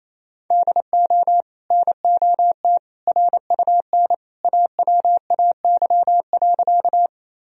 Morse code, heard as a keyed tone, about 28 wpm, 700 Hz. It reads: DO NOT RUN AWAY.
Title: Sentences from Top 1000 Words - Review Part 1 - 28wpm